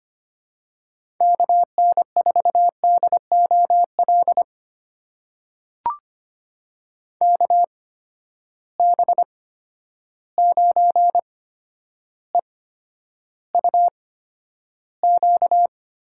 Morse code, heard as a keyed tone, about 25 wpm, spaced as for 11 wpm, 700 Hz. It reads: KN4DOL KB9EUQ
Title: USA Call Signs - 6 Characters - 8x Character Spacing - 25wpm